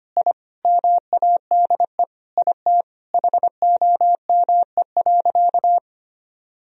Morse code, heard as a keyed tone, about 25 wpm, 700 Hz. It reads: I MADE IT HOME.
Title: Sentences from Top 1000 Words - Review Part 2 - 25wpm